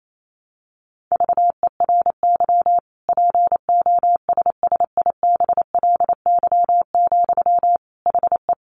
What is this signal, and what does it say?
Morse code, heard as a keyed tone, about 28 wpm, 700 Hz. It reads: VERY POSSIBLY, HE
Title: Sentences from Top 1300 Words 28wpm